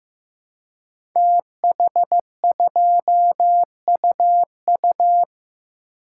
Morse code, heard as a keyed tone, about 15 wpm, 700 Hz. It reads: TH2UU
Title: Code Groups - 5 Letter:Numbers - 15wpm